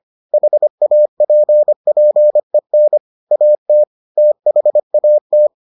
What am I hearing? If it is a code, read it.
Morse code, heard as a keyed tone, about 25 wpm, 600 Hz. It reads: HAPPEN AT THAT